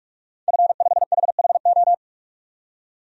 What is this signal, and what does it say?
Morse code, heard as a keyed tone, about 46 wpm, 700 Hz. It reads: F5HHO
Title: DXCC-100 Call Signs - 5 Characters - 45wpm